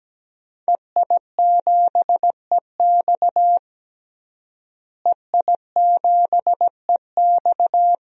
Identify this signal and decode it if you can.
Morse code, heard as a keyed tone, about 17 wpm, 700 Hz. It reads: EI7EX EI7EX
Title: DXCC-100 Call Signs - 5 Characters - Repeated 3x - 17wpm